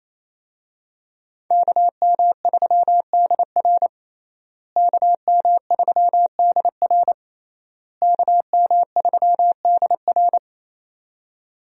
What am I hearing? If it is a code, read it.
Morse code, heard as a keyed tone, about 28 wpm, 700 Hz. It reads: KM3DR KM3DR KM3DR
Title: USA Call Signs - 5 Characters - Repeated 3x - 28wpm